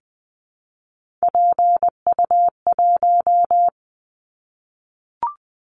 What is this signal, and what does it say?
Morse code, heard as a keyed tone, about 19 wpm, 700 Hz. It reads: PU1
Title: Callsign Prefixes for Top 100 DXCC - 20wpm